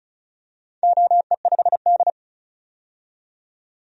Morse code, heard as a keyed tone, about 36 wpm, 700 Hz. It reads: OE5D
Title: DXCC-100 Call Signs - 4 Characters - 35wpm